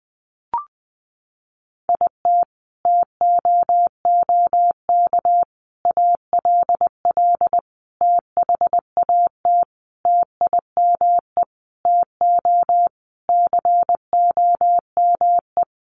Morse code, heard as a keyed tone, about 20 wpm, 700 Hz. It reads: IT TOOK ALL THAT TIME TO COME
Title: Sentences from Top 300 Words 20wpm